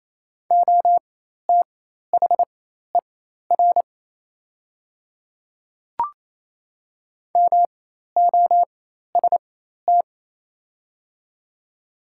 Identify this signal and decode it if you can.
Morse code, heard as a keyed tone, about 28 wpm, spaced as for 13 wpm, 700 Hz. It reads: OTHER MOST
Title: 4x Character Spacing - Top 100 Words - 28wpm